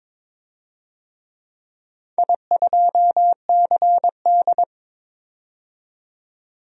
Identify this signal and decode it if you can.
Morse code, heard as a keyed tone, about 22 wpm, 700 Hz. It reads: I2CD
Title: DXCC-100 Call Signs - 4 Characters - 22wpm